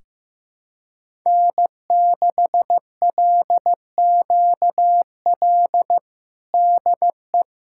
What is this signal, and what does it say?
Morse code, heard as a keyed tone, about 16 wpm, 700 Hz. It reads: N6LQL DE
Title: USA Callsign DE Callsign - 5 Characters - 15wpm